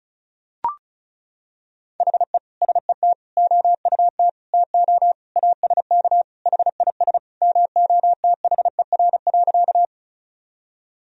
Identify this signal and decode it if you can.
Morse code, heard as a keyed tone, about 34 wpm, 700 Hz. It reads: HE SET OUT TO ASK HIS MOTHER.
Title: Sentences from Top 1000 Words - Review Part 1 - 35wpm